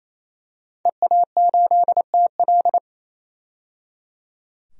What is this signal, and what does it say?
Morse code, heard as a keyed tone, about 28 wpm, 700 Hz. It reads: EA8TL
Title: DXCC-100 Call Signs - 5 Characters - 28wpm